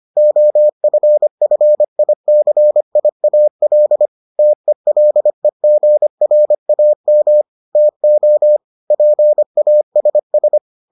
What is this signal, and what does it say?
Morse code, heard as a keyed tone, about 25 wpm, 600 Hz. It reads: OFFICIAL TELEGRAM TO PASS